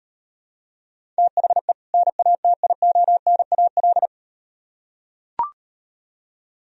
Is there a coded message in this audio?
Yes: Morse code from 1.2 to 4.1 s, heard as a keyed tone, about 38 wpm, 700 Hz. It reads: THE NATIONAL